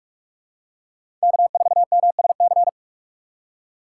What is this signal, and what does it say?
Morse code, heard as a keyed tone, about 43 wpm, 700 Hz. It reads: K4MSC
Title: USA Call Signs - 5 Characters - 3x Character Spacing - 45wpm